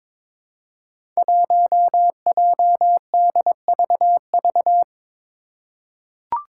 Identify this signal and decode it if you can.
Morse code, heard as a keyed tone, about 22 wpm, 700 Hz. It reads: 1JDVV
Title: Five Challenging Letter-Number Combinations 22wpm